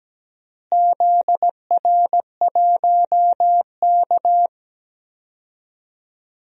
Morse code, heard as a keyed tone, about 17 wpm, 700 Hz. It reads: ZR1K
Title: DXCC-100 Call Signs - 4 Characters - Repeated 2x - 17wpm